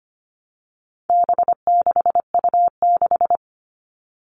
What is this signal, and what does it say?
Morse code, heard as a keyed tone, about 25 wpm, 700 Hz. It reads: B6U6
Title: All Cut Numbers - 4-Digit 25wpm